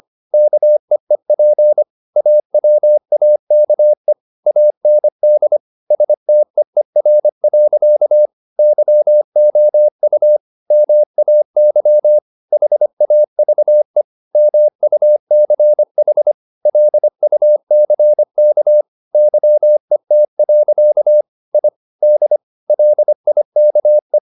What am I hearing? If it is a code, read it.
Morse code, heard as a keyed tone, about 25 wpm, 600 Hz. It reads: KEEP AWAKE AND STEER. YOU MAY HAVE MUCH LUCK YET. I D LIKE